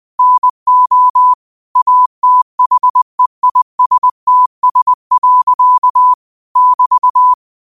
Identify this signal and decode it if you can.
Morse code, heard as a keyed tone, about 20 wpm, 1000 Hz. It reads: NO ATHEISTS. <BT>
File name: qotdfiles / 20wpm